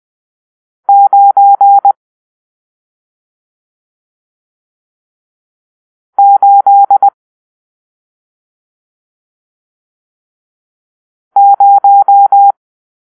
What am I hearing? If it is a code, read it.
Morse code, heard as a keyed tone, about 20 wpm, 800 Hz. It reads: 9 8 0
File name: arrl / Grp-7 Drill-1